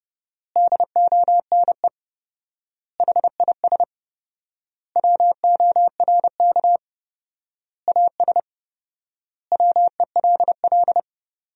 Morse code, heard as a keyed tone, about 30 wpm, 700 Hz. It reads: DONE HIS WORK AS WELL
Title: 4x Word-Spacing - Sentences from Top 500 Words - 30wpm